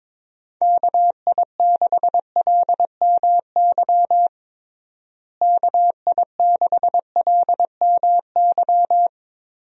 Morse code, heard as a keyed tone, about 22 wpm, 700 Hz. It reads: KI6LMY KI6LMY
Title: USA Call Signs - 6 Characters - Repeated 3x - 22wpm